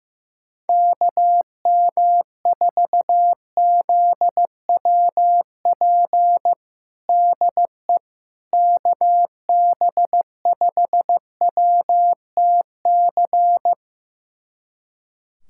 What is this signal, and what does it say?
Morse code, heard as a keyed tone, about 15 wpm, 700 Hz. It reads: KM4ZWP DE KB5WTC
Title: USA Callsign DE Callsign 15wpm